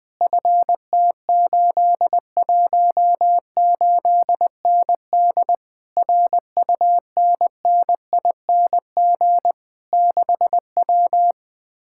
Morse code, heard as a keyed tone, about 21 wpm, 700 Hz. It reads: FT818ND RUNNING 6W
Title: Speed-Racing Common QSO Phrases Encore - 20wpm